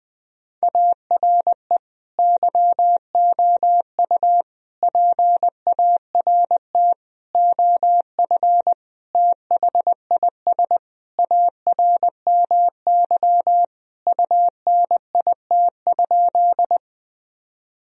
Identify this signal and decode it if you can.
Morse code, heard as a keyed tone, about 21 wpm, 700 Hz. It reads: ARE YOU PART OF THIS ARMY UNIT?
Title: Sentences from Top 1000 Words - Review Part 2 - 20wpm